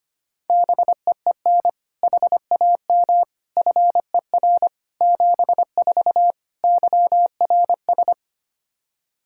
Morse code, heard as a keyed tone, about 25 wpm, 700 Hz. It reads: BEEN HAM FER 74 YRS
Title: K1USN SST QSO Practice 25wpm